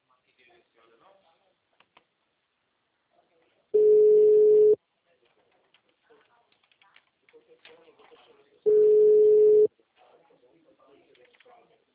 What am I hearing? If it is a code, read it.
Ringback tone (Europe, 425 Hz), with speech.